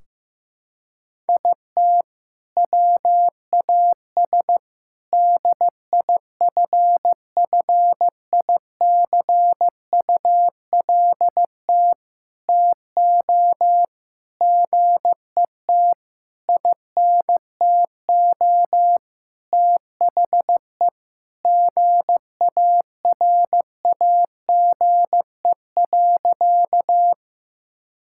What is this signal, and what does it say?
Morse code, heard as a keyed tone, about 15 wpm, 700 Hz. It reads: IT WAS DIFFICULT TO GET INTO THE GARAGE.
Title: Sentences from Top 1600 Words 15wpm